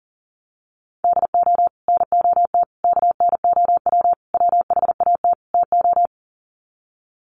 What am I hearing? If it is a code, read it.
Morse code, heard as a keyed tone, about 40 wpm, 700 Hz. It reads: DO NOT KNOW WHAT TO
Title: Sets of 5 Words 40wpm